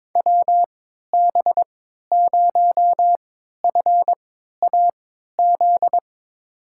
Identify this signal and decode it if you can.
Morse code, heard as a keyed tone, about 22 wpm, spaced as for 12 wpm, 700 Hz. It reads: WB0FAZ
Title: USA Call Signs - 6 Characters - 3x Character Spacing - 22wpm